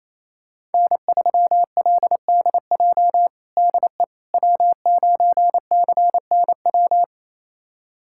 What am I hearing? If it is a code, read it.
Morse code, heard as a keyed tone, about 28 wpm, 700 Hz. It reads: N3LDJ DE W9CNW
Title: USA Callsign DE Callsign - 5 Characters - 28wpm